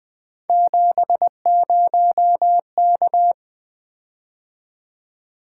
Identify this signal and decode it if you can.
Morse code, heard as a keyed tone, about 20 wpm, 700 Hz. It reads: 70K
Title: Three Letter-Number Combinations - 20wpm